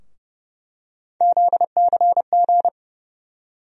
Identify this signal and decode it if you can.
Morse code, heard as a keyed tone, about 30 wpm, 700 Hz. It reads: ZCG
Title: USA Call Sign Postfixes - 3 Letters 30wpm